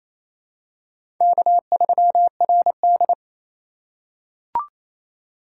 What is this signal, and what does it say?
Morse code, heard as a keyed tone, about 28 wpm, 700 Hz. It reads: K3RD
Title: USA Call Signs - 4 Characters - 2x Character Spacing - 28wpm